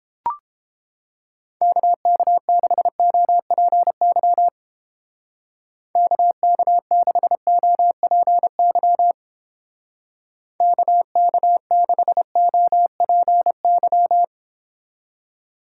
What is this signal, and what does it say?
Morse code, heard as a keyed tone, about 29 wpm, 700 Hz. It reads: KK6OPY KK6OPY KK6OPY